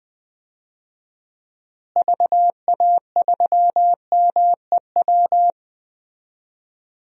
Morse code, heard as a keyed tone, about 20 wpm, 700 Hz. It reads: VA3MEW
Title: DXCC-100 Call Signs - 6 Characters - Repeated 3x - 20wpm